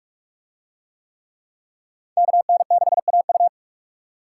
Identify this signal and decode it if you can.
Morse code, heard as a keyed tone, about 41 wpm, 700 Hz. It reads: KN6AU